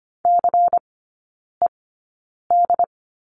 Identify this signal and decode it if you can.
Morse code, heard as a keyed tone, about 25 wpm, spaced as for 9 wpm, 700 Hz. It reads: CED
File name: course / 25:10wpm / 093 Lesson 16e - Letter c - Three Characters 25wpm